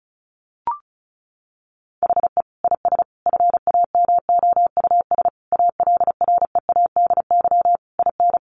Morse code, heard as a keyed tone, about 34 wpm, 700 Hz. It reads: HE IS FAMOUS ALREADY IN
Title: Sentences from Top 1500 Words 35wpm